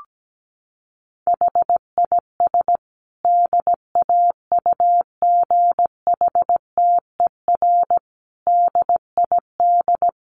Morse code, heard as a keyed tone, about 17 wpm, 700 Hz. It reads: HIS DAUGHTER DID
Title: Sentences from Top 1500 Words 17wpm